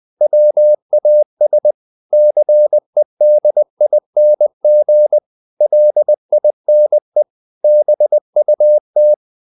Morse code, heard as a keyed tone, about 20 wpm, 600 Hz. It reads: WAS CEDING LINE BUT
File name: the old man and the sea part 2 20wpm